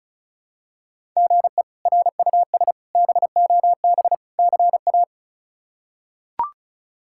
Morse code, heard as a keyed tone, about 34 wpm, 700 Hz. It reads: GE RUS BOB CA
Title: K1USN SST QSO Practice - 35wpm